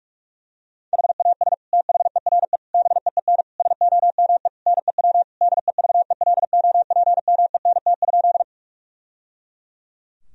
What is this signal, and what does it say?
Morse code, heard as a keyed tone, about 41 wpm, 700 Hz. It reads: HAS THERE BEEN SOME NEW DEVELOPMENT?
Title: Sentences from Top 1000 Words - Review Part 4 - 45wpm